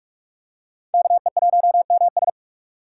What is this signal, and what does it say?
Morse code, heard as a keyed tone, about 42 wpm, 700 Hz. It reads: KE1MS